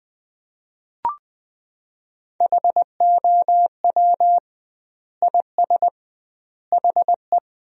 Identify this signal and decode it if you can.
Morse code, heard as a keyed tone, about 20 wpm, 700 Hz. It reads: HOW IS HE
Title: 2x Word-Spacing - Sentences from Top 100 Words - 20wpm